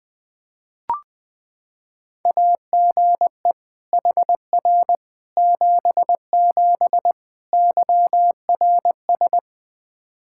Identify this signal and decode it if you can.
Morse code, heard as a keyed tone, about 21 wpm, 700 Hz. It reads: AGE HR 77 YRS
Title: Practice QSO - Exchange 3 - More Info - 20wpm